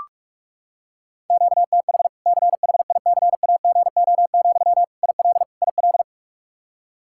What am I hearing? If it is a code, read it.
Morse code, heard as a keyed tone, about 46 wpm, 700 Hz. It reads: QTH CHICAGO, IL IL
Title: Practice QSO - Exchange 1 - Beginning - 45wpm